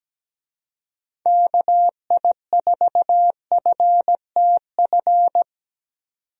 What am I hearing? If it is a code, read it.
Morse code, heard as a keyed tone, about 17 wpm, 700 Hz. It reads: KI4FTF
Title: USA Call Signs - 6 Characters - 17wpm